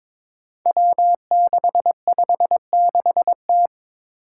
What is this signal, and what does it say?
Morse code, heard as a keyed tone, about 23 wpm, 700 Hz. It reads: W656T
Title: Five Challenging Letter-Number Combinations 22wpm